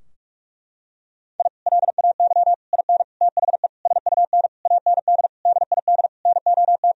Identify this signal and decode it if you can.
Morse code, heard as a keyed tone, about 46 wpm, 700 Hz. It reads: I LAY IN THE SUN AND DID NOT
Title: Sentences from Top 1000 Words - Review Part 2 - 45wpm